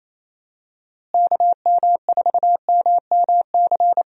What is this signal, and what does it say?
Morse code, heard as a keyed tone, about 29 wpm, 700 Hz. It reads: KM4MMC
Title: USA Call Signs - 6 Characters - Repeated 2x - 28wpm